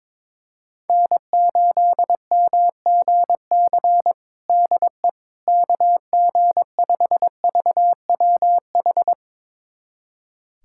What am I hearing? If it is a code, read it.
Morse code, heard as a keyed tone, about 22 wpm, 700 Hz. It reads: N8MGC DE KG5VWH
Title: USA Callsign DE Callsign 22wpm